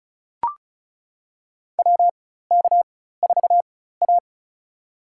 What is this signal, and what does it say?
Morse code, heard as a keyed tone, about 36 wpm, spaced as for 16 wpm, 700 Hz. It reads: WK4A
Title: USA Call Signs - 4 Characters - 4x Character Spacing - 35wpm